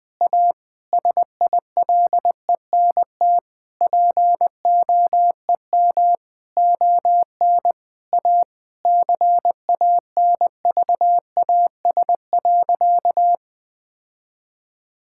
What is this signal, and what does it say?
Morse code, heard as a keyed tone, about 20 wpm, 700 Hz. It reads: A SILENT POEM ON A CANVAS.
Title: Difficult Sentences with Hints - 20wpm